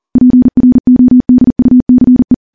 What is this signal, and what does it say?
Morse code, heard as a keyed tone, about 40 wpm, 250 Hz. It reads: PRODUCE
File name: morse / morse-training-40WPM